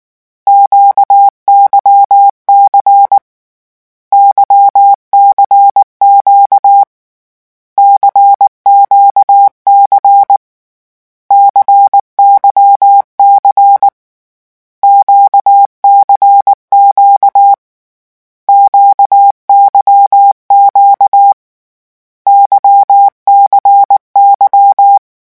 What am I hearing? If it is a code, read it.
Morse code, heard as a keyed tone, about 19 wpm, 800 Hz. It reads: QYC YCQ CQC CYC QCQ QYQ YCY
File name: morse mp3 / lecon2-vitesse 18